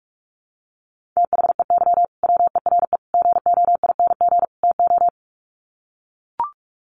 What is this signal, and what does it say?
Morse code, heard as a keyed tone, about 44 wpm, 700 Hz. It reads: THEY WERE GOING TO
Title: Sets of 4 Words 45wpm